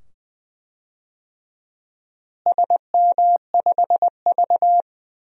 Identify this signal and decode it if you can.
Morse code, heard as a keyed tone, about 20 wpm, 700 Hz. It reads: SM5V
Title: DXCC-100 Call Signs - 4 Characters - Repeated 2x - 20wpm